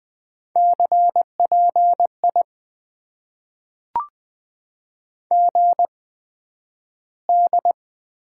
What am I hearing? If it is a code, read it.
Morse code, heard as a keyed tone, about 20 wpm, spaced as for 11 wpm, 700 Hz. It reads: CPI GD
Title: Top 90 QSO Elements - 8x Character Spacing - 20wpm